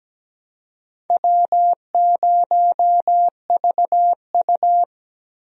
Morse code, heard as a keyed tone, about 17 wpm, 700 Hz. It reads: W0VU
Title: USA Call Signs - 4 Characters - Repeated 2x - 17wpm